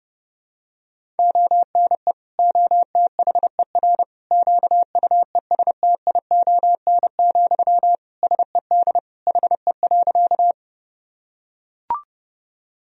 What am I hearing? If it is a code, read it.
Morse code, heard as a keyed tone, about 30 wpm, 700 Hz. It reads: ONE OTHER QUESTION, SED HE.
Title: Sentences from Top 400 Words and Abbreviations 30wpm